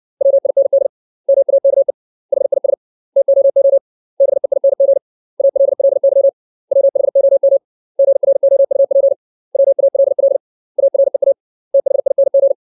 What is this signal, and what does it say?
Morse code, heard as a keyed tone, about 60 wpm, 550 Hz. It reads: WIND MAKE HIS TOO BEING ALLY WHOM GROUP WALL AREA THING
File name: 632 words / 632 Words 60WPM